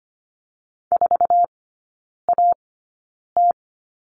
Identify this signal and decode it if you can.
Morse code, heard as a keyed tone, about 25 wpm, spaced as for 9 wpm, 700 Hz. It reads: 4AT
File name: course / 25:10wpm / 093 Lesson 16e - Letter c - Three Characters 25wpm